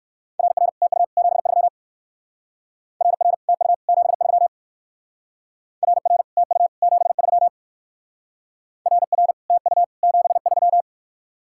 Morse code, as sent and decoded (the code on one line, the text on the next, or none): .-. .-. / - ..- / --... ...-- / .-. .-. / - ..- / --... ...-- / .-. .-. / - ..- / --... ...-- / .-. .-. / - ..- / --... ...--
RR TU 73 RR TU 73 RR TU 73 RR TU 73